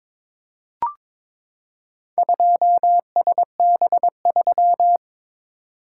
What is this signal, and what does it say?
Morse code, heard as a keyed tone, about 23 wpm, 700 Hz. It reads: 2SB3
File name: practice / 22wpm / Four Letter-Number Combinations - 22wpm